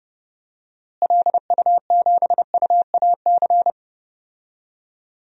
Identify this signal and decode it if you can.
Morse code, heard as a keyed tone, about 30 wpm, 700 Hz. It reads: LU7UAC
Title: DXCC-100 Call Signs - 6 Characters - Repeated 3x - 30wpm